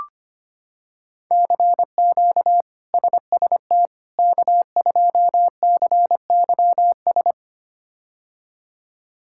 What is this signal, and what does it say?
Morse code, heard as a keyed tone, about 25 wpm, 700 Hz. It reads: CQ SST K2CYS
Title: K1USN SST QSO Practice - 25wpm